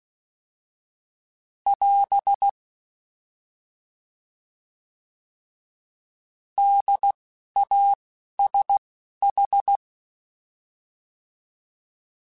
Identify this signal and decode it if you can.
Morse code, heard as a keyed tone, about 16 wpm, 800 Hz. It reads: <AS> D A S H
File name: Alphabetical / Alphabetical mp3s / Ds @10